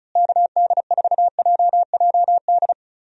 Morse code, heard as a keyed tone, about 36 wpm, 700 Hz. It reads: KD4JJD
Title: USA Call Signs - 6 Characters - Repeated 2x - 35wpm